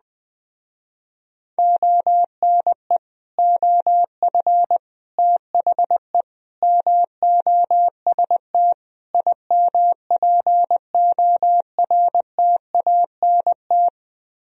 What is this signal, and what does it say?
Morse code, heard as a keyed tone, about 21 wpm, 700 Hz. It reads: ONE OF THE MOST IMPORTANT